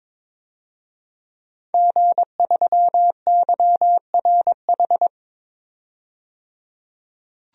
Morse code, heard as a keyed tone, about 21 wpm, 700 Hz. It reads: G3YRH
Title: DXCC-100 Call Signs - 5 Characters - 22wpm